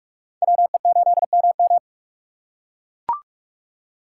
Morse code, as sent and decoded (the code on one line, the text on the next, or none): .-- . ---.. -- --
WE8MM